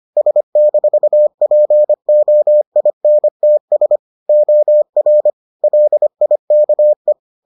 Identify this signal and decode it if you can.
Morse code, heard as a keyed tone, about 25 wpm, 600 Hz. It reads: S-POINTS OR LIKE